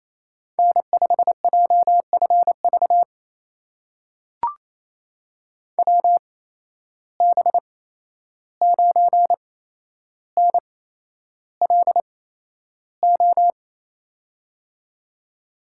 Morse code, heard as a keyed tone, about 28 wpm, spaced as for 11 wpm, 700 Hz. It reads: N5JFV WB9NLO
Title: USA Call Signs - 8x Character Spacing - 28wpm